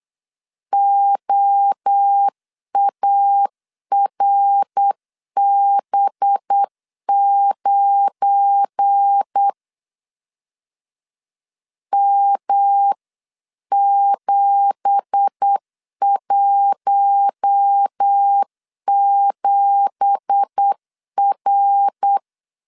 Morse code, heard as a keyed tone, about 8 wpm, 800 Hz. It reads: OARB9 M717R
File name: CW / DET / 1 9 1500C S A 8wpm 5637